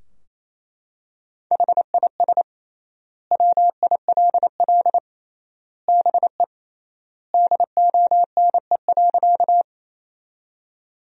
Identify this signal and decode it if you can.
Morse code, heard as a keyed tone, about 28 wpm, 700 Hz. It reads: HIS WILL BE DONE.
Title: 3x Word-Spacing - Sentences from Top 500 Words - 28wpm